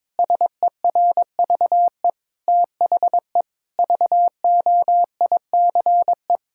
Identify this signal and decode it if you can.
Morse code, heard as a keyed tone, about 22 wpm, 700 Hz. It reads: SERVE THE VOICE
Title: Sentences from Top 1000 Words - Review Part 6 - 22wpm